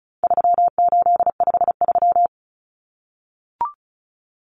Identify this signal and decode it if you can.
Morse code, heard as a keyed tone, about 34 wpm, 700 Hz. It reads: 3853
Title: Common Cut Numbers - 4-Digit 35wpm